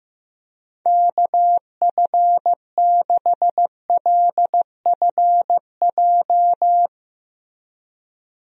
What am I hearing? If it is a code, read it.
Morse code, heard as a keyed tone, about 16 wpm, 700 Hz. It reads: KF6LFJ
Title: USA Call Signs - 6 Characters - Repeated 2x - 15wpm